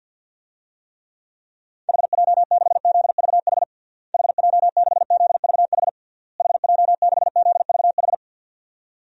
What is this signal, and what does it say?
Morse code, heard as a keyed tone, about 50 wpm, 700 Hz. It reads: HJ6ZVH HJ6ZVH HJ6ZVH
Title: DXCC-100 Call Signs - 6 Characters - Repeated 3x - 50wpm